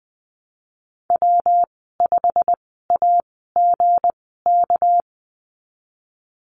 Morse code, heard as a keyed tone, about 20 wpm, spaced as for 14 wpm, 700 Hz. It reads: W5AGK
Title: 238 Lesson 38g - Letter q - Call Signs 20wpm